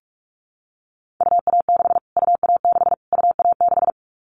Morse code, heard as a keyed tone, about 44 wpm, 700 Hz. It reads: UA6 UA6 UA6
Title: Callsign Prefixes for Top 25 DXCC - Repeated 3x - 45wpm